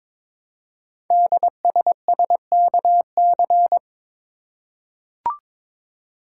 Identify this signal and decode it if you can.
Morse code, heard as a keyed tone, about 22 wpm, 700 Hz. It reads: DSSKC
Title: Code Groups - 5 Letters - 22wpm